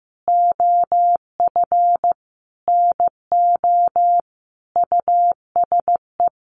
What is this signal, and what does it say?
Morse code, heard as a keyed tone, about 14 wpm, 700 Hz. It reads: OF NO USE
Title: Sentences from Top 1400 Words 15wpm